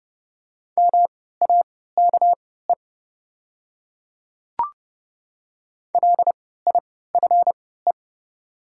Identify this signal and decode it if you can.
Morse code, heard as a keyed tone, about 30 wpm, spaced as for 17 wpm, 700 Hz. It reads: MAKE LIFE
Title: 3x Character Spacing - Top 300 Words - 30wpm